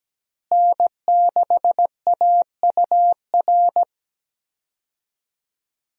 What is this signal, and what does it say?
Morse code, heard as a keyed tone, about 17 wpm, 700 Hz. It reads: N6AUR
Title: USA Call Signs - 5 Characters - Repeated 3x - 17wpm